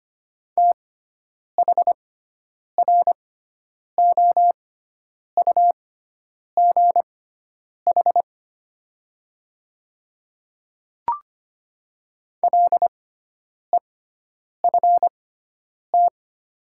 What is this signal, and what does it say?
Morse code, heard as a keyed tone, about 25 wpm, spaced as for 9 wpm, 700 Hz. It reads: THROUGH LEFT
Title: 6x Character Spacing - Top 300 Words - 25wpm